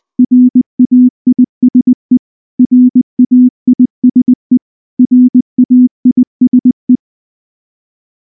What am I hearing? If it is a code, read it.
Morse code, heard as a keyed tone, about 19 wpm, 250 Hz. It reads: RAISE RAISE RAISE
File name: morse / morse-training-20WPM-7